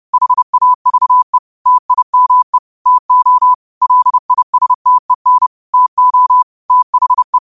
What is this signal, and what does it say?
Morse code, heard as a keyed tone, about 30 wpm, 1000 Hz. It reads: HAVE TIME TO LISTEN TO THE
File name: qotdfiles / 30wpm